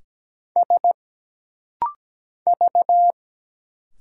Morse code, heard as a keyed tone, about 17 wpm, 700 Hz. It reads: S V
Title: Single Letters - Rapid-Fire 17wpm